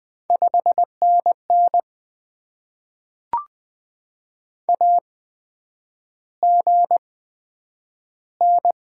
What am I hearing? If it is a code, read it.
Morse code, heard as a keyed tone, about 20 wpm, spaced as for 9 wpm, 700 Hz. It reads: 5NN AGN
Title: Top 90 QSO Elements - 8x Character Spacing - 20wpm